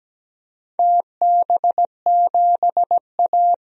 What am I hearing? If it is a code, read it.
Morse code, heard as a keyed tone, about 17 wpm, 700 Hz. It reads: TB7A
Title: DXCC-100 Call Signs - 4 Characters - 17wpm